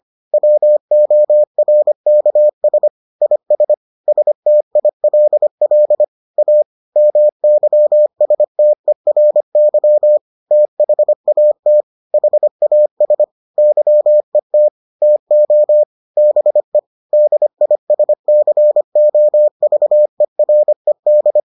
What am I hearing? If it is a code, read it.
Morse code, heard as a keyed tone, about 25 wpm, 600 Hz. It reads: WORKS IS STILL A MYSTERY THAT HAS YET TO BE DISCOVERED